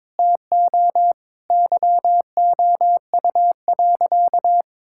Morse code, heard as a keyed tone, about 22 wpm, 700 Hz. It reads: TO YOU.
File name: Sentences with 7-Word Phrases - 22wpm